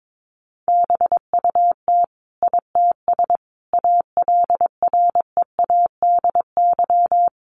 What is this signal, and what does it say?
Morse code, heard as a keyed tone, about 22 wpm, 700 Hz. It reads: BUT ITS ALREADY
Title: Sentences from Top 700 Words 22wpm